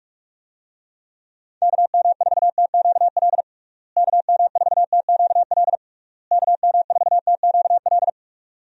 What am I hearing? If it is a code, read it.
Morse code, heard as a keyed tone, about 42 wpm, 700 Hz. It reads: KM4TQL KM4TQL KM4TQL
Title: USA Call Signs - 6 Characters - Repeated 3x - 45wpm